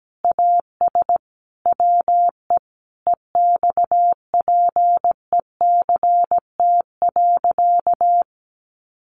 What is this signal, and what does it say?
Morse code, heard as a keyed tone, about 17 wpm, 700 Hz. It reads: AS WE EXPECT.
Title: Sentences from Top 1100 Words 17wpm